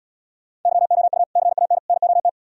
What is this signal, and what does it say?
Morse code, heard as a keyed tone, about 77 wpm, 700 Hz. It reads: YOU CAN NOT